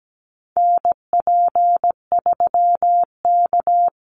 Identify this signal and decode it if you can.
Morse code, heard as a keyed tone, about 17 wpm, 700 Hz. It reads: NP3K
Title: Call Signs Repeated 3x - 17wpm